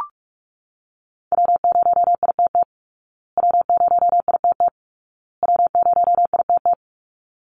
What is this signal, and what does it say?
Morse code, heard as a keyed tone, about 44 wpm, 700 Hz. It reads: W0ITT W0ITT W0ITT
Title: Call Signs Repeated 2x - 45wpm